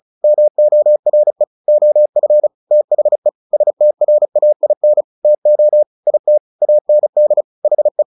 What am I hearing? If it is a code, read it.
Morse code, heard as a keyed tone, about 34 wpm, 600 Hz. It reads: MORE OF THE STRAIN TO IT AND HE
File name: the old man and the sea / the old man and the sea 1 35wpm